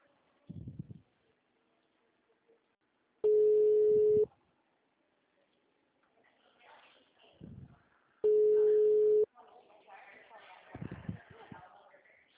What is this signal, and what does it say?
Ringback tone (Europe, 425 Hz), with speech.
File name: out- -2005- - - .567